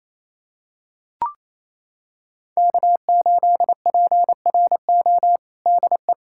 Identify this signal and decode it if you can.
Morse code, heard as a keyed tone, about 28 wpm, 700 Hz. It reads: K8PRO DE
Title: USA Callsign DE Callsign - 5 Characters - 28wpm